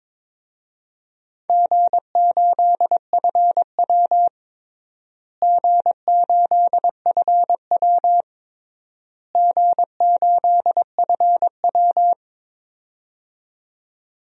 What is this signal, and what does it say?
Morse code, heard as a keyed tone, about 22 wpm, 700 Hz. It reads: G8FW G8FW G8FW
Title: DXCC-100 Call Signs - 4 Characters - Repeated 3x - 22wpm